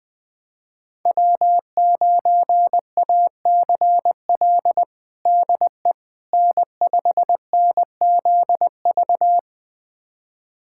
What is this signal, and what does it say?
Morse code, heard as a keyed tone, about 20 wpm, 700 Hz. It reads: W9ACL DE N5NZV
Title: USA Callsign DE Callsign - 5 Characters - 20wpm